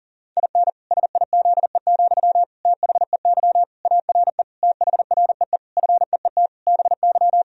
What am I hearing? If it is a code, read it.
Morse code, heard as a keyed tone, about 37 wpm, 700 Hz. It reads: IN SIZE, THEY ARE THREE FEET BY
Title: Sentences from Top 1000 Words - Review Part 2 - 40wpm